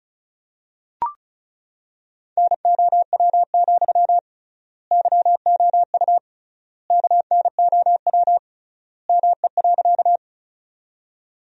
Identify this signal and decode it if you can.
Morse code, heard as a keyed tone, about 34 wpm, 700 Hz. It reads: NOW, YOU KNOW ME.
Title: 3x Word-Spacing - Sentences from Top 200 Words - 35wpm